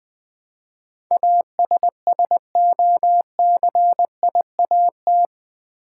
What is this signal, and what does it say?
Morse code, heard as a keyed tone, about 20 wpm, 700 Hz. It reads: ASSOCIAT
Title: Build A Word - 4 Letters At A Time - 10-Letter Words - 20wpm